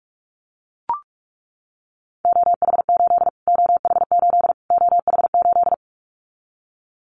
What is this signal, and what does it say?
Morse code, heard as a keyed tone, about 44 wpm, 700 Hz. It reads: OH8 OH8 OH8
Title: Callsign Prefixes for Top 25 DXCC - Repeated 3x - 45wpm